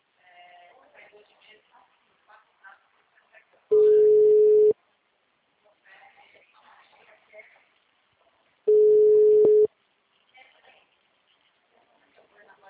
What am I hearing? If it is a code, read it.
Ringback tone (Europe, 425 Hz), with speech.
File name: out- -2025- - - .19251